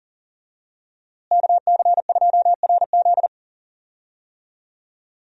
Morse code, heard as a keyed tone, about 40 wpm, 700 Hz. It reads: KC2RZ
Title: USA Call Signs - 5 Characters - Repeated 2x - 40wpm